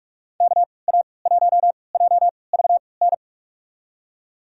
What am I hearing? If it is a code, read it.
Morse code, heard as a keyed tone, about 45 wpm, spaced as for 25 wpm, 700 Hz. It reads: KA1JVN
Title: USA Call Signs - 3x Character Spacing - 45wpm